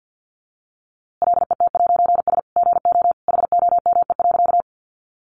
Morse code, heard as a keyed tone, about 50 wpm, 700 Hz. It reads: LET'S GO HOME.
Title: A Sound Of Thunder 50wpm